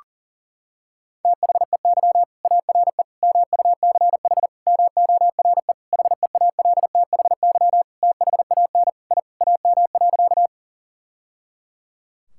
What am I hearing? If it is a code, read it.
Morse code, heard as a keyed tone, about 40 wpm, 700 Hz. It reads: THEY ARE MUCH MORE HEALTHY THAN I AM.
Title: Sentences from Top 1000 Words - Review Part 5 - 40wpm